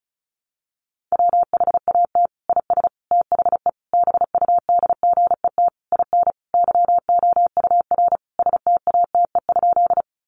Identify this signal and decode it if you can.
Morse code, heard as a keyed tone, about 34 wpm, 700 Hz. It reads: WHAT IS THE BUDGET IN YOUR STATE?
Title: Sentences from Top 1200 Words 35wpm